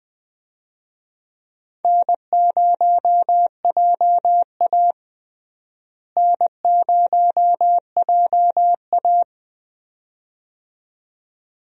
Morse code, heard as a keyed tone, about 20 wpm, 700 Hz. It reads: N0JA N0JA
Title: USA Call Signs - 4 Characters - Repeated 2x - 20wpm